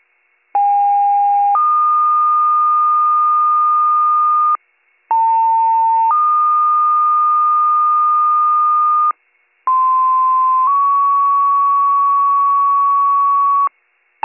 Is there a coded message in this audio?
Yes: two-tone page from 0.5 to 13.7 s, heard as a one-second tone, then a three-second tone, about 810 then 1220 Hz, about 900 then 1220 Hz, about 1040 then 1120 Hz.